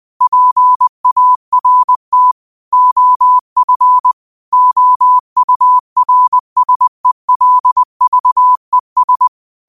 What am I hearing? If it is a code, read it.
Morse code, heard as a keyed tone, about 20 wpm, 1000 Hz. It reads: PART OF OURSELVES
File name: 20wpm